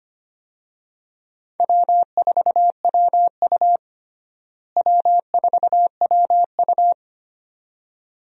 Morse code, heard as a keyed tone, about 25 wpm, 700 Hz. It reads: W4WU W4WU
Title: USA Call Signs - 4 Characters - Repeated 2x - 25wpm